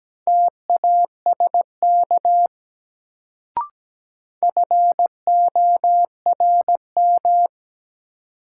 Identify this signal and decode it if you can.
Morse code, heard as a keyed tone, about 17 wpm, 700 Hz. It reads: TASK FORM
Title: Sentences from Top 1100 Words - New Word Drill Alt - 17wpm